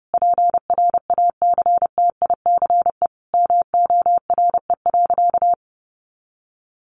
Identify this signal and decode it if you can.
Morse code, heard as a keyed tone, about 30 wpm, 700 Hz. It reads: PRACTICE MORE.
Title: Sentences from Top 700 Words 30wpm